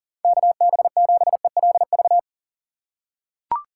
Morse code, heard as a keyed tone, about 40 wpm, 700 Hz. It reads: KB7ELV
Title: USA Call Signs - 6 Characters - 3x Character Spacing - 40wpm